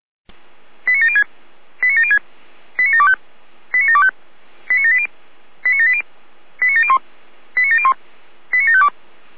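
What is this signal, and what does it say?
Five-tone selective call (ZVEI), heard as quick runs of five tones, tones of about 70 ms. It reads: 89706 89706 89725 89725 89780 89780 89701 89701 89752
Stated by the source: Five-tone ZVEI selective call 89436 received by SDR.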